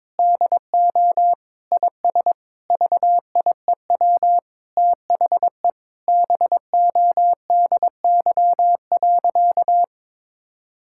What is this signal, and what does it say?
Morse code, heard as a keyed tone, about 22 wpm, 700 Hz. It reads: DO IS VIEW THE BODY.
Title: Sentences from Top 1000 Words - Review Part 6 - 22wpm